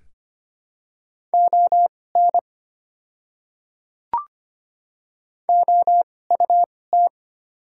Morse code, heard as a keyed tone, about 25 wpm, spaced as for 18 wpm, 700 Hz. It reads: ON OUT
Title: 2x Character Spacing - Top 100 Words - 25wpm